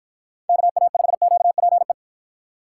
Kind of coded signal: Morse code

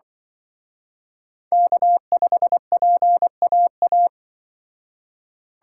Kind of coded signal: Morse code

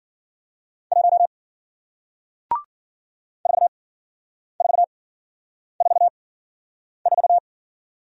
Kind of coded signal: Morse code

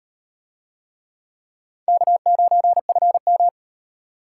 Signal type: Morse code

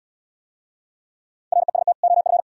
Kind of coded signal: Morse code

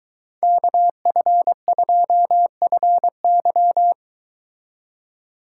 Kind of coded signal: Morse code

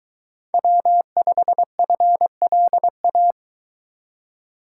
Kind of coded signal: Morse code

